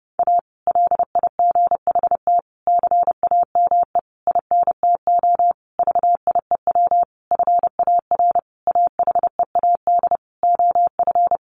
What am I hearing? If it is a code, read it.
Morse code, heard as a keyed tone, about 30 wpm, 700 Hz. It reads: A LIGHT CAME INTO VIEW FAR AHEAD OF
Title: Sentences from Top 1100 Words 30wpm